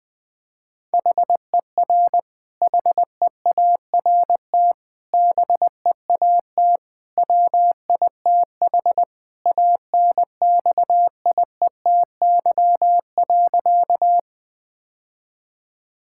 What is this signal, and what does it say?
Morse code, heard as a keyed tone, about 21 wpm, 700 Hz. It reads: HER HEART BEAT WITH ANXIETY.
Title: Sentences from Top 1600 Words 20wpm